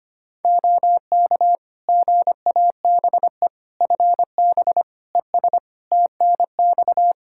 Morse code, heard as a keyed tone, about 25 wpm, 700 Hz. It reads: OK GABE FB ES TNX
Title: Practice QSO - Exchange 2 - Info - 25wpm